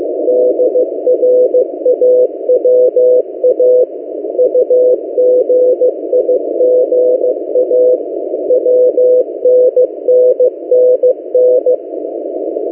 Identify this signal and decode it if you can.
Morse code, heard as a keyed tone, about 15 wpm, 500 Hz. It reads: DRAWA UGIGA WNNNN
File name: M08a- -1355utc-8096khz